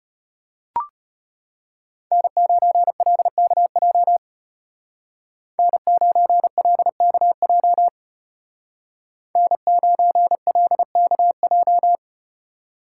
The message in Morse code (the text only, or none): N9LKJ N9LKJ N9LKJ